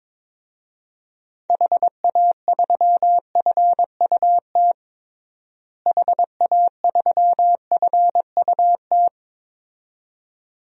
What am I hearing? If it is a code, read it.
Morse code, heard as a keyed tone, about 22 wpm, 700 Hz. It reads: HA3FUT HA3FUT
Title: DXCC-100 Call Signs - 6 Characters - Repeated 2x - 22wpm